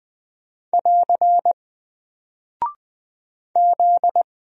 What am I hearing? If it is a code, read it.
Morse code, heard as a keyed tone, about 20 wpm, 700 Hz. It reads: <AR> Z
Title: Single Character (Letter:Number:Punctuation:Prosign) Alt 20wpm